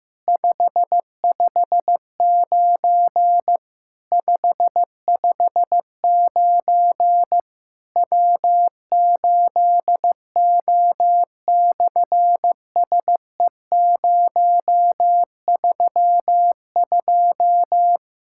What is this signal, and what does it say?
Morse code, heard as a keyed tone, about 14 wpm, 700 Hz. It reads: 559 559 W8O/SE032
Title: SOTA QSO Practice 15wpm